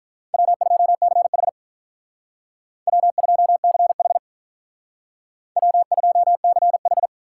Morse code, heard as a keyed tone, about 43 wpm, 700 Hz. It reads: W2CH W2CH W2CH
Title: Speed Racing - USA 1x2 Call Signs 35wpm